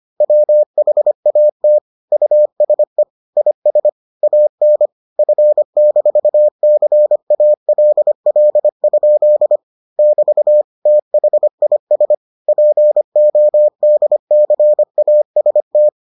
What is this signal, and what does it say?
Morse code, heard as a keyed tone, about 25 wpm, 600 Hz. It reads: WHAT USE IS AN F-CALL? <BT> THIS PODCAST